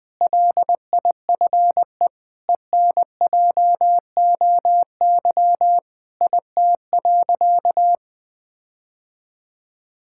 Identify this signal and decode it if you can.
Morse code, heard as a keyed tone, about 20 wpm, 700 Hz. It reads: LIFE ENJOY IT.
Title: Sentences with 6-Word Phrases - 20wpm